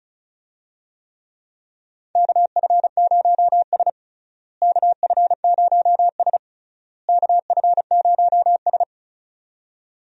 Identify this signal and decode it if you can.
Morse code, heard as a keyed tone, about 36 wpm, 700 Hz. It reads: KF0S KF0S KF0S
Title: USA Call Signs - 4 Characters - Repeated 3x - 35wpm